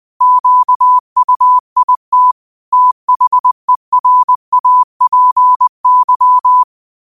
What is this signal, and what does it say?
Morse code, heard as a keyed tone, about 20 wpm, 1000 Hz. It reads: QUIT THERAPY
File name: qotdfiles / 20wpm